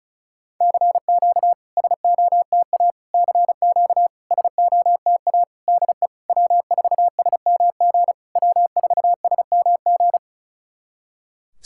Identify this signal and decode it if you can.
Morse code, heard as a keyed tone, about 36 wpm, 700 Hz. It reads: CQ SOTA CQ SOTA DE W4SMG W4SMG
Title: SOTA QSO Practice 35wpm